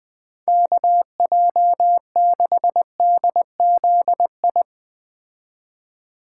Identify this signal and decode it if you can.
Morse code, heard as a keyed tone, about 20 wpm, 700 Hz. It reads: KJ6DZI